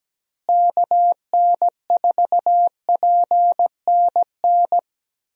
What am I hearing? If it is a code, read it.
Morse code, heard as a keyed tone, about 17 wpm, 700 Hz. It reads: KN4PNN
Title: USA Call Signs - 6 Characters - Repeated 3x - 17wpm